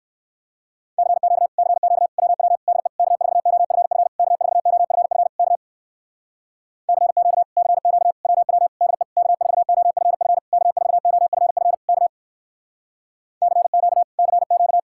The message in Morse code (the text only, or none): CQ CQ WW DE K4OFF K4OFF K CQ CQ WW DE K4OFF K4OFF K CQ CQ